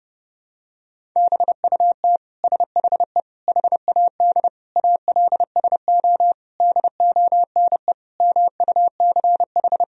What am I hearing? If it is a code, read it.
Morse code, heard as a keyed tone, about 32 wpm, 700 Hz. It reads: BUT SHE HAD ALSO DONE MUCH
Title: Sentences from Top 1000 Words - Review Part 1 - 30wpm